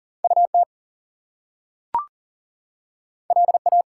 Morse code, heard as a keyed tone, about 40 wpm, 700 Hz. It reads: UT LA
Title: US States and Canadian Provinces 40wpm